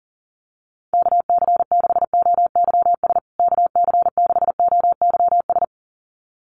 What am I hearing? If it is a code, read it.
Morse code, heard as a keyed tone, about 40 wpm, 700 Hz. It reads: KC6OYS KC6OYS
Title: USA Call Signs Repeated 2x 40wpm